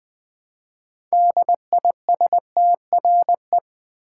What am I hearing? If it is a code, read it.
Morse code, heard as a keyed tone, about 20 wpm, 700 Hz. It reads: DISTRE
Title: Build A Word - 3 Letters At A Time - 8-Letter Words - 20wpm